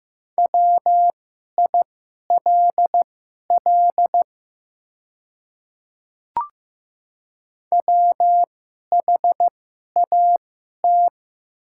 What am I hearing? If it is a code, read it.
Morse code, heard as a keyed tone, about 15 wpm, spaced as for 11 wpm, 690 Hz. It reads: WILL WHAT
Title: 2x Character Spacing - Top 500 Words - 15wpm